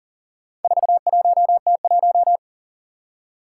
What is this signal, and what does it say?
Morse code, heard as a keyed tone, about 43 wpm, 700 Hz. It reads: 41T1
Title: Four Challenging Letter-Number Combinations 40wpm